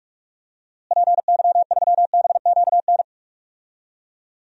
Morse code, heard as a keyed tone, about 46 wpm, 700 Hz. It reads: PY3BQN
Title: DXCC-100 Call Signs - 6 Characters - Repeated 2x - 45wpm